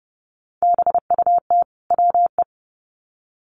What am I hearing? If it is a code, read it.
Morse code, heard as a keyed tone, about 30 wpm, 700 Hz. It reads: BUT WE
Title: Sets of 2 Words 30wpm